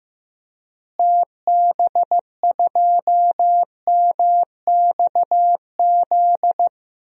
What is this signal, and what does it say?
Morse code, heard as a keyed tone, about 15 wpm, 700 Hz. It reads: TB2MXZ
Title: DXCC-100 Call Signs - 6 Characters - Repeated 2x - 15wpm